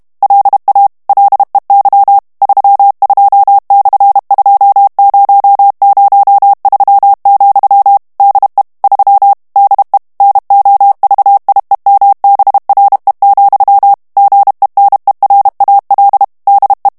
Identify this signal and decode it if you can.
Morse code, heard as a keyed tone, about 32 wpm, 800 Hz. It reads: LA LEY 32/2003, DE 3 DE NOVIEMBRE, GENERAL DE
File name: morse mp3 / ESPAGNOL TEXTE EN CLAIR 2 VITESSE 30 MOTS MINUTES